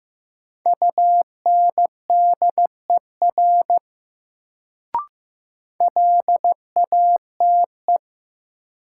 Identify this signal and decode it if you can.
Morse code, heard as a keyed tone, about 16 wpm, 700 Hz. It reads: UNDER LATE
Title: Sentences from Top 300 Words - New Word Drill - 15wpm